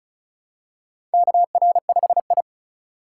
Morse code, heard as a keyed tone, about 36 wpm, 700 Hz. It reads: KR5I
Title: USA Call Signs - 4 Characters - 2x Character Spacing - 35wpm